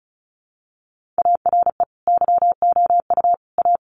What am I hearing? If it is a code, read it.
Morse code, heard as a keyed tone, about 34 wpm, 700 Hz. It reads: ARE YOU A
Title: Sentences from Top 1500 Words 35wpm